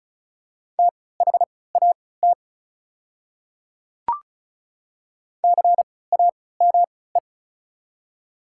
Morse code, heard as a keyed tone, about 36 wpm, spaced as for 20 wpm, 700 Hz. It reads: THAT CAME
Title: 3x Character Spacing - Top 200 Words - 35wpm